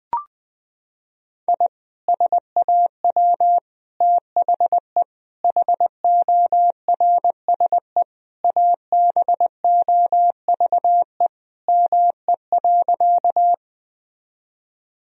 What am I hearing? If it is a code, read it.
Morse code, heard as a keyed tone, about 21 wpm, 700 Hz. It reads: I SAW THE HORSE ABOVE ME.
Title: Sentences from Top 1000 Words - Review Part 2 - 20wpm